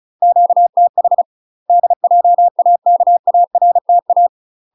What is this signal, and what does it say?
Morse code, heard as a keyed tone, about 36 wpm, 700 Hz. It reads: QTH DJAKARTA
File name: UK MASTER QSOs TRAINING / 35WPM / UK MASTER QSOs 0020